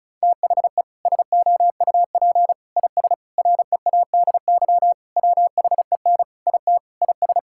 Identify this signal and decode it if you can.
Morse code, heard as a keyed tone, about 36 wpm, 700 Hz. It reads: THE SOUP IS READY WHEN IT IS
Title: Sentences from Top 1600 Words 35wpm